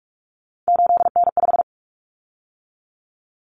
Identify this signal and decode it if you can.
Morse code, heard as a keyed tone, about 44 wpm, 700 Hz. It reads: 8N5
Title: Common Cut Numbers - 3-Digit 45wpm